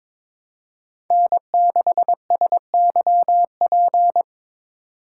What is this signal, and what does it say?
Morse code, heard as a keyed tone, about 22 wpm, 700 Hz. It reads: N6SYP
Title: USA Call Signs - 3x Character Spacing - 22wpm